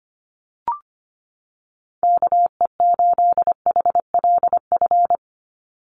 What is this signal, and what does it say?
Morse code, heard as a keyed tone, about 25 wpm, 700 Hz. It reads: KE8HLF
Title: USA Call Signs 25wpm